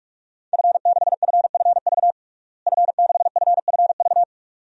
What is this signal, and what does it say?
Morse code, heard as a keyed tone, about 47 wpm, 700 Hz. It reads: F6FFV F6FFV
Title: DXCC-100 Call Signs - 5 Characters - Repeated 2x - 45wpm